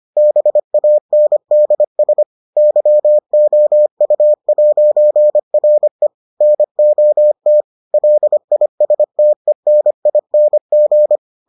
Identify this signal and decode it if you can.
Morse code, heard as a keyed tone, about 25 wpm, 600 Hz. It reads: BANDS YOU'RE NOT LISTENING